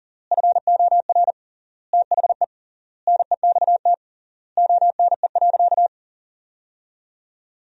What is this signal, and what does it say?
Morse code, heard as a keyed tone, about 37 wpm, 700 Hz. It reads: FOR THE NEXT ONE.
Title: 3x Word-Spacing - Sentences from Top 400 Words - 40wpm